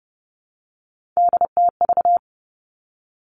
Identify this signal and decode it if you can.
Morse code, heard as a keyed tone, about 32 wpm, 700 Hz. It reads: DTV
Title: Three Challenging Letter-Number Combinations 30wpm